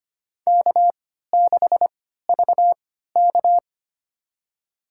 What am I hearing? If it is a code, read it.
Morse code, heard as a keyed tone, about 25 wpm, spaced as for 14 wpm, 700 Hz. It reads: K6VK
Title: USA Call Signs - 3x Character Spacing - 25wpm